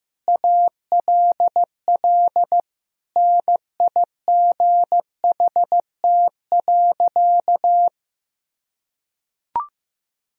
Morse code, heard as a keyed tone, about 15 wpm, 700 Hz. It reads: ALL NIGHT.
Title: Sentences from Top 500 Words and Abbreviations 15wpm